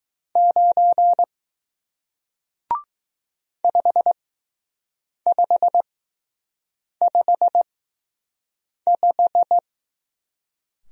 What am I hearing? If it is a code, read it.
Morse code, heard as a keyed tone, about 20 wpm, spaced as for 6 wpm, 700 Hz. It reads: 9 5555